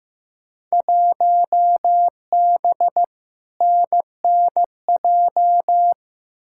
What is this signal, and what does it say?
Morse code, heard as a keyed tone, about 15 wpm, 700 Hz. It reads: 1B NNJ
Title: Field Day Exchanges Encore - 15wpm